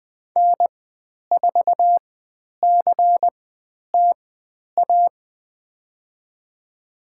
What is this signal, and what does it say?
Morse code, heard as a keyed tone, about 20 wpm, spaced as for 10 wpm, 700 Hz. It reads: N4CTA
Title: 095 Lesson 16g - Letter c - Call Signs 20wpm